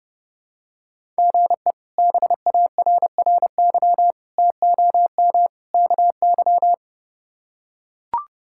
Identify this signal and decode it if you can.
Morse code, heard as a keyed tone, about 30 wpm, 700 Hz. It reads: GE BARRY TOM KY
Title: K1USN SST QSO Practice - 30wpm